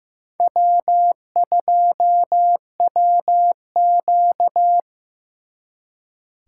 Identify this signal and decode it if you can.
Morse code, heard as a keyed tone, about 18 wpm, 700 Hz. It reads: W2WQ
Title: Speed Racing - USA 1x2 Call Signs 15wpm